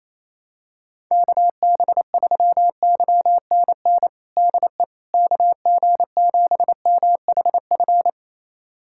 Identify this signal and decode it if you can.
Morse code, heard as a keyed tone, about 28 wpm, 700 Hz. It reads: KB3YNN DE KG7MHF
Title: USA Callsign DE Callsign - 6 Characters - 28wpm